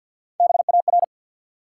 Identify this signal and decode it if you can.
Morse code, heard as a keyed tone, about 48 wpm, 700 Hz. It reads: BAR
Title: Three Letter Words - 50wpm